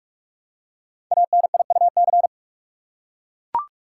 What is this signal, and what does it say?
Morse code, heard as a keyed tone, about 44 wpm, 700 Hz. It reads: ANIUC
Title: Code Groups - 5 Letter:Numbers - 45wpm